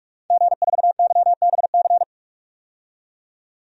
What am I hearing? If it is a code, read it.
Morse code, heard as a keyed tone, about 43 wpm, 700 Hz. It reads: G4YBC
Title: DXCC-100 Call Signs - 5 Characters - 45wpm